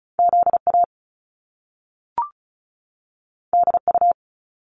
Morse code, heard as a keyed tone, about 36 wpm, 700 Hz. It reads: Z A D U
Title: Top Two Letter Combinations in Words 35wpm